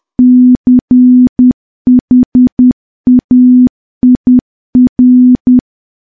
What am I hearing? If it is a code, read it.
Morse code, heard as a keyed tone, about 10 wpm, 250 Hz. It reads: CHAIR
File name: morse / morse-training-10WPM